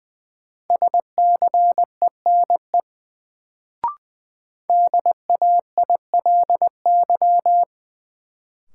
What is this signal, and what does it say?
Morse code, heard as a keyed tone, about 20 wpm, 700 Hz. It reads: SCENE DAILY
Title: Sentences from Top 1100 Words - New Word Drill Alt - 20wpm